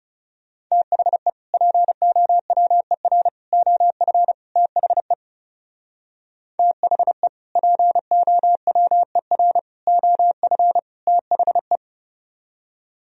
Morse code, heard as a keyed tone, about 36 wpm, 700 Hz. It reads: THE POWER OF THE THE POWER OF THE
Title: Speed-Racing - Sets of 4 Words 30wpm